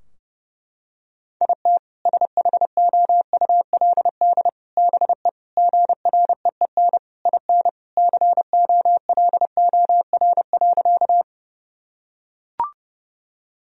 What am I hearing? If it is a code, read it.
Morse code, heard as a keyed tone, about 32 wpm, 700 Hz. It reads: IT SHOULD BE GREEN IN COLOR.
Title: Sentences from Top 1000 Words - Review Part 2 - 30wpm